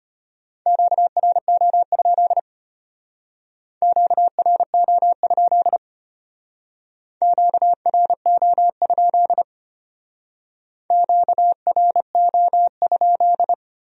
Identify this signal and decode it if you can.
Morse code, heard as a keyed tone, about 31 wpm, 700 Hz. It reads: QRO? QRO? QRO? QRO?
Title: Speed-Racing Q-Codes - 25wpm